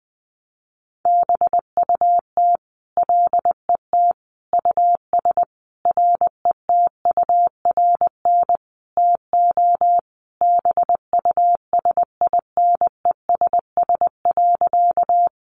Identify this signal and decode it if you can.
Morse code, heard as a keyed tone, about 19 wpm, 700 Hz. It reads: BUT LET US RETURN TO BUSINESS.
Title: Sentences from Top 800 Words 20wpm